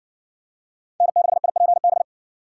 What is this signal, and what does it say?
Morse code, heard as a keyed tone, about 60 wpm, 700 Hz. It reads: N6IPB